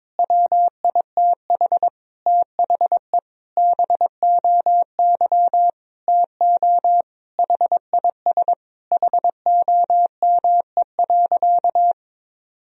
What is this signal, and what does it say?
Morse code, heard as a keyed tone, about 22 wpm, 700 Hz. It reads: WITH THE BOY TO HIS HOME.
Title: Sentences from Top 1000 Words - Review Part 1 - 22wpm